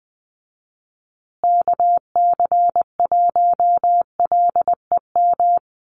Morse code, heard as a keyed tone, about 20 wpm, 700 Hz. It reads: KC1LEM
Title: USA Call Signs Repeated 3x 20wpm